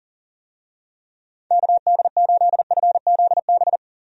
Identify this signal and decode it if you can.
Morse code, heard as a keyed tone, about 40 wpm, 690 Hz. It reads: KD8FZB
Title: USA Call Signs - 6 Characters - 40wpm